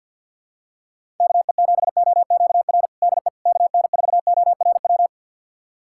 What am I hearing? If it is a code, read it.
Morse code, heard as a keyed tone, about 52 wpm, 700 Hz. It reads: KE7OQR DE KN4ORW
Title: USA Callsign DE Callsign - 6 Characters - 50wpm